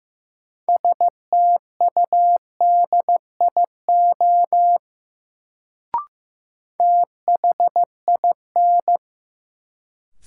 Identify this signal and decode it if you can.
Morse code, heard as a keyed tone, about 15 wpm, 700 Hz. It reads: STUDIO THIN
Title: Sentences from Top 1200 Words - New Word Drill Alt - 15wpm